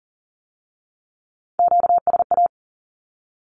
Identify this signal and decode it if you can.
Morse code, heard as a keyed tone, about 40 wpm, 700 Hz. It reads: QSA
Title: Q-Codes with Spoken Meaning - 40wpm